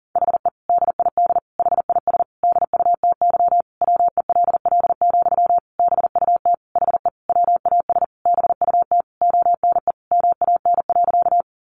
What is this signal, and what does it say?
Morse code, heard as a keyed tone, about 40 wpm, 700 Hz. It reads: HE DID HIS DUTY WELL, BUT HE WAS BUT ONE MAN.
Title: Sentences from Top 1500 Words 40wpm